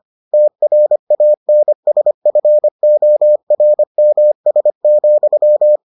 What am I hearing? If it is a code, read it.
Morse code, heard as a keyed tone, about 25 wpm, 600 Hz. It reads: TRANSFORMS,